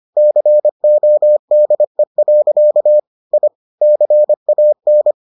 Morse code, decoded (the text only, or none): CODE. I CAN